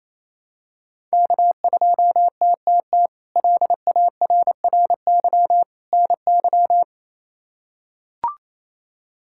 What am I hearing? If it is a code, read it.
Morse code, heard as a keyed tone, about 28 wpm, 700 Hz. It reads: K2TTT LARRY NY
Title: K1USN SST QSO Practice - 28wpm